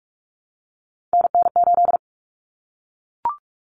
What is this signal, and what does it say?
Morse code, heard as a keyed tone, about 44 wpm, 700 Hz. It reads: NN8
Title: Common Cut Numbers - 3-Digit 45wpm